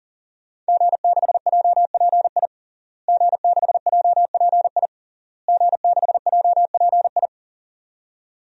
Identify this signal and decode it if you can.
Morse code, heard as a keyed tone, about 40 wpm, 700 Hz. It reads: G6JPI G6JPI G6JPI
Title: DXCC-100 Call Signs - 5 Characters - Repeated 3x - 40wpm